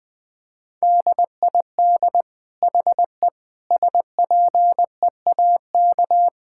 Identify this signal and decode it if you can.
Morse code, heard as a keyed tone, about 20 wpm, 700 Hz. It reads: DID HE SPEAK
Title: Sentences from Top 1000 Words - Review Part 6 - 20wpm